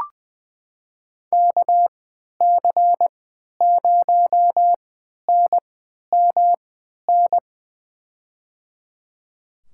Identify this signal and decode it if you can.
Morse code, heard as a keyed tone, about 20 wpm, spaced as for 11 wpm, 700 Hz. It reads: KC0NMN
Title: USA Call Signs - 3x Character Spacing - 20wpm